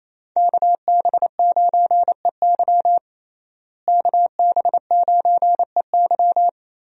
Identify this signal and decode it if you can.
Morse code, heard as a keyed tone, about 28 wpm, 700 Hz. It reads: KB9EY KB9EY
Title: USA Call Signs - 5 Characters - Repeated 2x - 28wpm